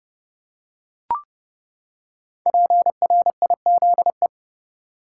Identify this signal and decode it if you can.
Morse code, heard as a keyed tone, about 32 wpm, 700 Hz. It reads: PRIZE
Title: Five Letter Words - 30wpm